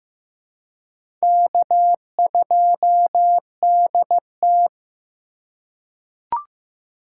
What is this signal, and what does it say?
Morse code, heard as a keyed tone, about 15 wpm, 700 Hz. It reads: K2DT
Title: USA Call Signs - 4 Characters - 3x Character Spacing - 15wpm